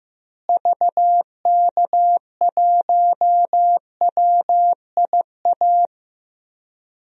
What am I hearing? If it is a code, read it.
Morse code, heard as a keyed tone, about 15 wpm, 700 Hz. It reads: VK1WIA
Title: DXCC-100 Call Signs - 6 Characters - Repeated 3x - 15wpm